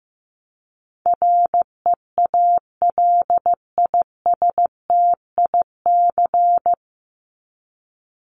Morse code, heard as a keyed tone, about 15 wpm, 700 Hz. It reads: REALISTIC
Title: Top 1751-2000 Words 15wpm